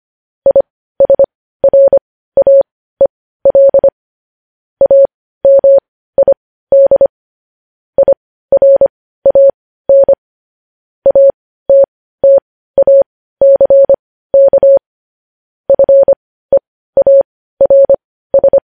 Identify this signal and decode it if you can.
Morse code, heard as a keyed tone, about 25 wpm, spaced as for 15 wpm, 550 Hz. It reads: ISRAEL AMID IRAN ATTACK FEARS
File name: podcast / News headlines-1x-25x15